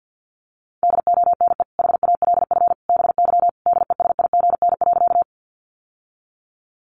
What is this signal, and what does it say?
Morse code, heard as a keyed tone, about 50 wpm, 700 Hz. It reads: DONE HALF BY DESIGN.
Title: Sentences from Top 800 Words 50wpm